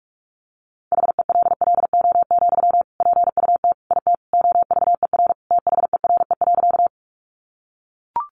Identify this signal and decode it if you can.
Morse code, heard as a keyed tone, about 44 wpm, 700 Hz. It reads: HELLO, PUT IT OVER THERE.
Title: Sentences from Top 1100 Words 45wpm